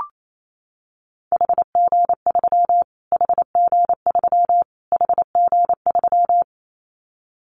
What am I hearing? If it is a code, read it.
Morse code, heard as a keyed tone, about 28 wpm, 700 Hz. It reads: HG3 HG3 HG3
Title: Callsign Prefixes for Top 25 DXCC - Repeated 3x - 28wpm